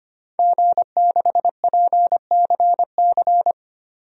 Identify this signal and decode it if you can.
Morse code, heard as a keyed tone, about 25 wpm, 700 Hz. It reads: G6PCC
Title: DXCC-100 Call Signs - 5 Characters - 25wpm